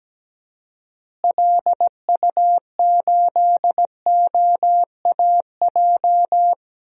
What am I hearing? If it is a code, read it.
Morse code, heard as a keyed tone, about 17 wpm, 700 Hz. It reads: LU8OAJ
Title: DXCC-100 Call Signs - 6 Characters - Repeated 2x - 17wpm